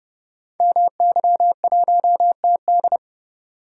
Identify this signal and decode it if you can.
Morse code, heard as a keyed tone, about 30 wpm, 700 Hz. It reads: MY1TD
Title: Code Groups - 5 Letter:Numbers - 30wpm